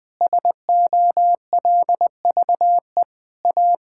Morse code, heard as a keyed tone, about 20 wpm, 700 Hz. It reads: SOLVE A
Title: Sentences with 9-Word Phrases - 20wpm